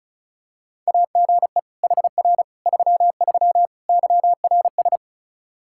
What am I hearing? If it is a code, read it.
Morse code, heard as a keyed tone, about 36 wpm, 700 Hz. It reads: AGE HR 33 YRS
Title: Practice QSO - Exchange 3 - More Info - 35wpm